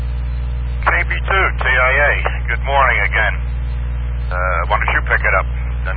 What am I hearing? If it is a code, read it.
Emergency Alert System (SAME) tones (header bursts), heard as harsh buzzing bursts, with speech. It is heard from 1.0 to 5.3 s.